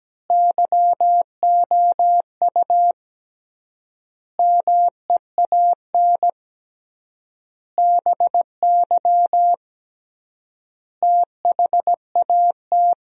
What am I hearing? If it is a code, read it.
Morse code, heard as a keyed tone, about 17 wpm, 700 Hz. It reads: YOU MEAN BY THAT
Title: 3x Word-Spacing - Sentences from Top 400 Words - 17wpm